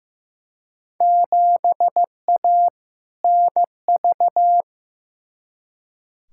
Morse code, heard as a keyed tone, about 15 wpm, 700 Hz. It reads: 7A NV
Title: Field Day Exchanges Encore - 15wpm